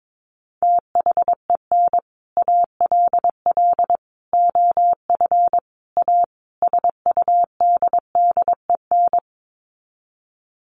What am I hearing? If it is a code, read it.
Morse code, heard as a keyed tone, about 22 wpm, 700 Hz. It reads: THEN ALL OF A SUDDEN
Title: Sets of 5 Words Encore 22wpm